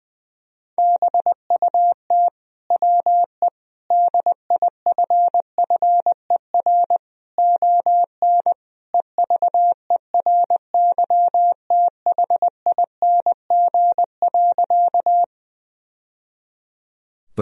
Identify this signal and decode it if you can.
Morse code, heard as a keyed tone, about 20 wpm, 700 Hz. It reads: BUT WE DIFFER ON EVERYTHING.
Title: Sentences from Top 1000 Words - Review Part 1 - 20wpm